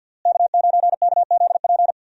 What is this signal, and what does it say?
Morse code, heard as a keyed tone, about 51 wpm, 700 Hz. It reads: K9KZP
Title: USA Call Signs - 5 Characters - Repeated 3x - 50wpm